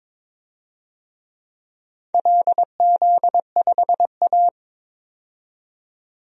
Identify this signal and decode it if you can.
Morse code, heard as a keyed tone, about 22 wpm, 700 Hz. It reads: LZ5A
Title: DXCC-100 Call Signs - 4 Characters - 22wpm